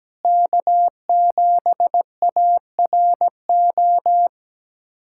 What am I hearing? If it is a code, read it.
Morse code, heard as a keyed tone, about 17 wpm, 700 Hz. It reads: K7ARO
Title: USA Callsign DE Callsign - 5 Characters - 17wpm